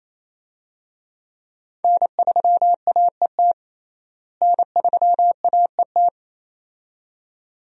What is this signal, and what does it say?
Morse code, heard as a keyed tone, about 27 wpm, 700 Hz. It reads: N3AET N3AET
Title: USA Call Signs - 5 Characters - Repeated 2x - 28wpm